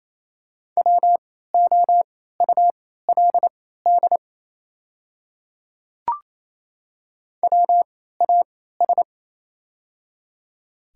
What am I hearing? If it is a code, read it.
Morse code, heard as a keyed tone, about 28 wpm, spaced as for 16 wpm, 700 Hz. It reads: WOULD WAS
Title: 3x Character Spacing - Top 100 Words - 28wpm